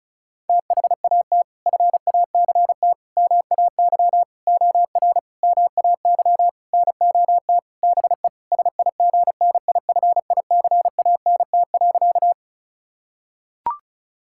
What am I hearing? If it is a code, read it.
Morse code, heard as a keyed tone, about 36 wpm, 700 Hz. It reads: THAT FACT MAY OR MAY NOT BE SIGNIFICANT.
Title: Sentences from Top 1000 Words - Review Part 6 - 35wpm